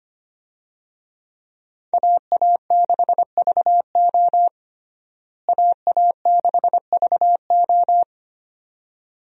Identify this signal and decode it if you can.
Morse code, heard as a keyed tone, about 25 wpm, 700 Hz. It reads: AA6VO AA6VO
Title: USA Call Signs - 5 Characters - Repeated 2x - 25wpm